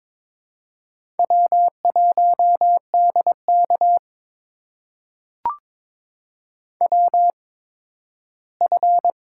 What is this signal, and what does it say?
Morse code, heard as a keyed tone, about 22 wpm, spaced as for 13 wpm, 700 Hz. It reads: W1DK WF
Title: USA Call Signs - 4 Characters - 8x Character Spacing - 22wpm